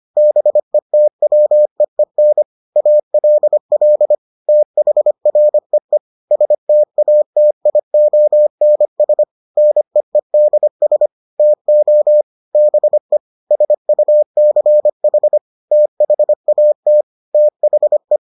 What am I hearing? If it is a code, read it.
Morse code, heard as a keyed tone, about 25 wpm, 600 Hz. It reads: BETWEEN ALL THREE STATIONS NEEDS TO BE SUCH THAT THE